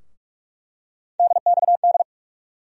Morse code, heard as a keyed tone, about 47 wpm, 700 Hz. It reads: DXD
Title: USA Call Sign Postfixes - 3 Letters 45wpm